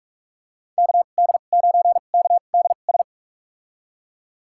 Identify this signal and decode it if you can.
Morse code, heard as a keyed tone, about 46 wpm, spaced as for 32 wpm, 700 Hz. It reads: KD9KDS
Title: USA Call Signs - 6 Characters - 2x Character Spacing - 45wpm